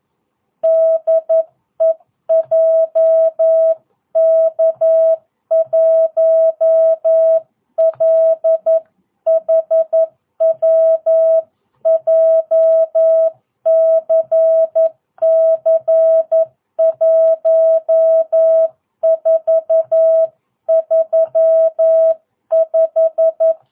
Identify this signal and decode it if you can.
Morse code, heard as a keyed tone, about 11 wpm, 650 Hz. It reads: DE JK1LHW JCC 1435
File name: jh8cbh / 2022cw